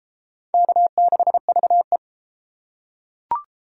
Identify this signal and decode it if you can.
Morse code, heard as a keyed tone, about 30 wpm, 700 Hz. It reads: K6VE